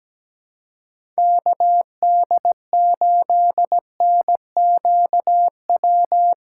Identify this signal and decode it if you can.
Morse code, heard as a keyed tone, about 17 wpm, 700 Hz. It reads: KD8NQW
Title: USA Call Signs - 6 Characters - 2x Character Spacing - 17wpm